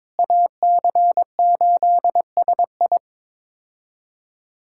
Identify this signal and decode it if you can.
Morse code, heard as a keyed tone, about 22 wpm, 700 Hz. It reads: AC8SI
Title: K1USN SST QSO Practice - 22wpm